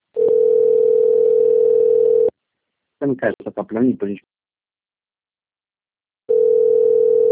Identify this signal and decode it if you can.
Ringback tone (North America), 440 + 480 Hz, with speech.
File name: rg-1600-unknown- - - .97524